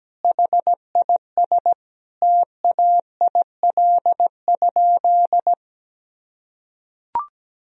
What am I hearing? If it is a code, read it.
Morse code, heard as a keyed tone, about 17 wpm, 700 Hz. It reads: HIS TAIL?
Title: Sentences from Top 1000 Words - Review Part 3 - 17wpm